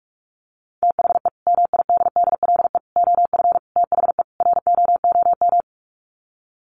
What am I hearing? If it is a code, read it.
Morse code, heard as a keyed tone, about 44 wpm, 700 Hz. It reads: THE MIDDLE OF THE ROOM
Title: Sets of 5 Words Encore2 45wpm